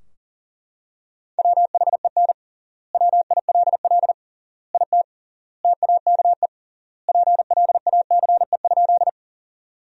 Morse code, heard as a keyed tone, about 40 wpm, 700 Hz. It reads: WHEN WILL IT TAKE PLACE?
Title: 3x Word-Spacing - Sentences from Top 200 Words - 40wpm